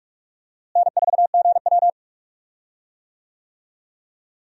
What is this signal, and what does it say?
Morse code, heard as a keyed tone, about 46 wpm, 700 Hz. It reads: N4GW
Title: USA Call Signs - 4 Characters - Repeated 2x - 45wpm